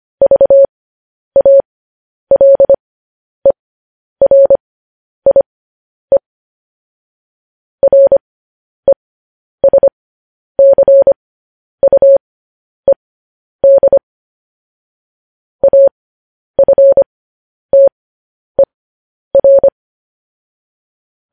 Morse code, heard as a keyed tone, about 25 wpm, spaced as for 10 wpm, 550 Hz. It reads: VALERIE RESCUED AFTER
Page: news headlines in morse code at 10 wpm